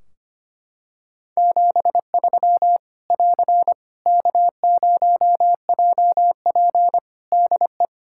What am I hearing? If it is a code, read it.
Morse code, heard as a keyed tone, about 25 wpm, 700 Hz. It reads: 73 <AR> K0JP DE
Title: Practice QSO - Exchange 4 - Ending - 25wpm